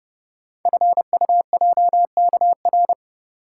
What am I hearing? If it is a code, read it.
Morse code, heard as a keyed tone, about 30 wpm, 700 Hz. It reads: FUJKR
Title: Code Groups - 5 Letters - 30wpm